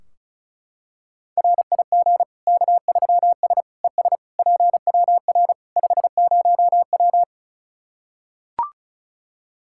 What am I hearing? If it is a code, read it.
Morse code, heard as a keyed tone, about 34 wpm, 700 Hz. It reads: RIG K3S ES PWR 50W
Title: K1USN SST QSO Practice 35wpm